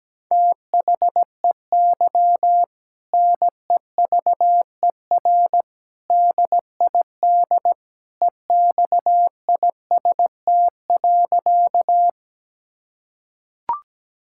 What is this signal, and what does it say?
Morse code, heard as a keyed tone, about 17 wpm, 700 Hz. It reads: THEY NEVER DID EXIST.
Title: Sentences from Top 1000 Words - Review Part 5 - 17wpm